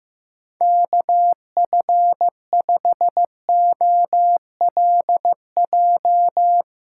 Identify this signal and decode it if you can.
Morse code, heard as a keyed tone, about 15 wpm, 700 Hz. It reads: KF5OLJ
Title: USA Call Signs - 6 Characters - 3x Character Spacing - 15wpm